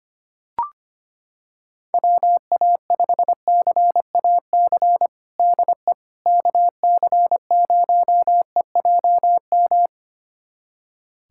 Morse code, heard as a keyed tone, about 25 wpm, 700 Hz. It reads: WA5CAC DE KC0EJM
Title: USA Callsign DE Callsign 25wpm